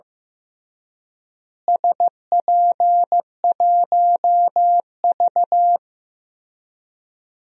Morse code, heard as a keyed tone, about 15 wpm, 700 Hz. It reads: SP1V
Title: DXCC-100 Call Signs - 4 Characters - Repeated 3x - 15wpm